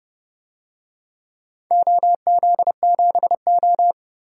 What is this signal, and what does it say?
Morse code, heard as a keyed tone, about 28 wpm, 700 Hz. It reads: OZ7O
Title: DXCC-100 Call Signs - 4 Characters - 30wpm